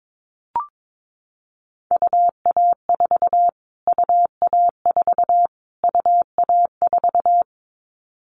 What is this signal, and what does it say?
Morse code, heard as a keyed tone, about 22 wpm, 700 Hz. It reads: UA4 UA4 UA4
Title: Callsign Prefixes for Top 25 DXCC - Repeated 3x - 22wpm